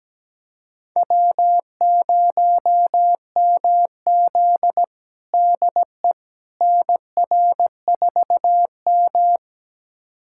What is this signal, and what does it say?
Morse code, heard as a keyed tone, about 17 wpm, 700 Hz. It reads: W0MZ DE NR4M
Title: USA Callsign DE Callsign - 4 Characters - 17wpm